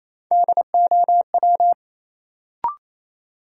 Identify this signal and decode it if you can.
Morse code, heard as a keyed tone, about 27 wpm, 700 Hz. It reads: DOW
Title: Sets of 5 Words - New Word Drill - 28wpm